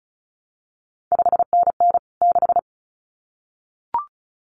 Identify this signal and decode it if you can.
Morse code, heard as a keyed tone, about 36 wpm, 700 Hz. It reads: 5NN 6
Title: CQ WW DX Contest Exchange Encore - 35wpm